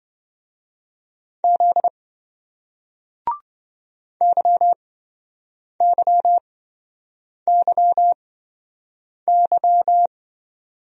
Morse code, heard as a keyed tone, about 25 wpm, spaced as for 7 wpm, 700 Hz. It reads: Z YYYY
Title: Speed-Racing Single Letters 20wpm